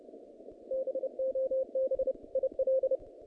Morse code, heard as a keyed tone, about 30 wpm, 530 Hz. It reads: BOB IL
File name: contests / naqp-cw-k5tr-op-k5pi- -2200